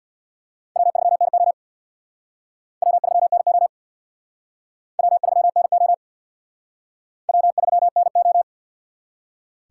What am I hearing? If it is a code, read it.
Morse code, heard as a keyed tone, about 64 wpm, 700 Hz. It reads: W3NO W3NO W3NO W3NO